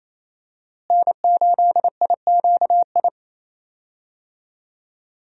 Morse code, heard as a keyed tone, about 28 wpm, 700 Hz. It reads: N8IQI
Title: USA Call Signs - 5 Characters - Repeated 2x - 28wpm